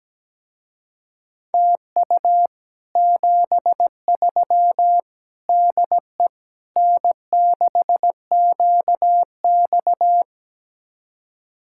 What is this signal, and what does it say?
Morse code, heard as a keyed tone, about 17 wpm, 700 Hz. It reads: TU 73 DE N6QX
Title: Speed-Racing Common QSO Phrases Encore - 17wpm